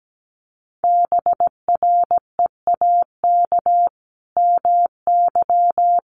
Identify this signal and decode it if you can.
Morse code, heard as a keyed tone, about 17 wpm, 700 Hz. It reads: BREAK MY
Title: Sentences from Top 1400 Words 17wpm